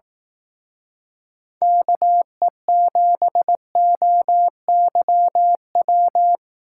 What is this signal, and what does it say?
Morse code, heard as a keyed tone, about 18 wpm, 700 Hz. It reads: KE7OYW
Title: Speed Racing - USA 2x3 Call Signs 15wpm